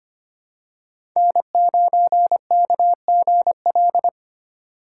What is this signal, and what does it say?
Morse code, heard as a keyed tone, about 25 wpm, 700 Hz. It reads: N9KGL
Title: USA Call Signs - 5 Characters - 4x Character Spacing - 25wpm